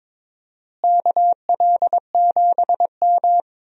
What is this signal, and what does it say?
Morse code, heard as a keyed tone, about 21 wpm, 700 Hz. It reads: KL7M
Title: DXCC-100 Call Signs - 4 Characters - Repeated 3x - 22wpm